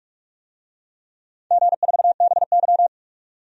Morse code, heard as a keyed tone, about 47 wpm, 700 Hz. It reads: G4BY
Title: DXCC-100 Call Signs - 4 Characters - 45wpm